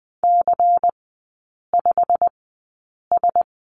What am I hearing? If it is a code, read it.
Morse code, heard as a keyed tone, about 20 wpm, spaced as for 8 wpm, 700 Hz. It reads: C5S
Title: 100 Lesson 17e - Letter u - Three Characters 20wpm